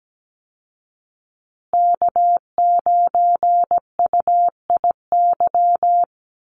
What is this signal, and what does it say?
Morse code, heard as a keyed tone, about 17 wpm, 700 Hz. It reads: K9UIY
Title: Call Signs Repeated 2x - 17wpm